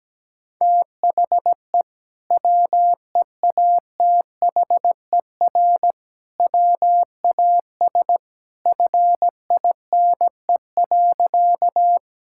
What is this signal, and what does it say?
Morse code, heard as a keyed tone, about 17 wpm, 700 Hz. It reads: THE WEATHER WAS FINE.
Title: Sentences from Top 1000 Words - Review Part 5 - 17wpm